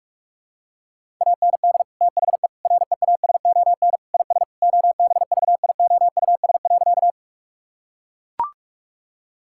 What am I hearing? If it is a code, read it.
Morse code, heard as a keyed tone, about 41 wpm, 700 Hz. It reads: AND THE REASON IS OBVIOUS.
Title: Sentences from Top 1600 Words 45wpm